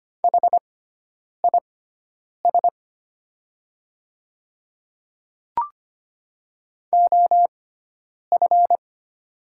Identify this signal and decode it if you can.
Morse code, heard as a keyed tone, about 25 wpm, spaced as for 9 wpm, 700 Hz. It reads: HIS OF
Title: 6x Character Spacing - Top 100 Words - 25wpm